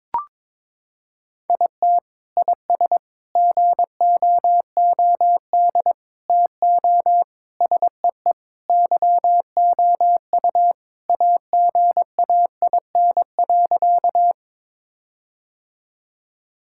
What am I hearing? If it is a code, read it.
Morse code, heard as a keyed tone, about 22 wpm, 700 Hz. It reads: IT IS GOOD TO SEE YOU AGAIN.
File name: practice / 22wpm / Sentences with 4-Word Phrases - 22wpm